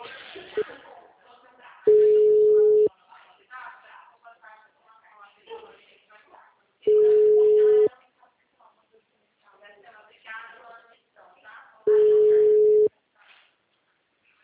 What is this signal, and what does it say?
Ringback tone (Europe, 425 Hz), with speech.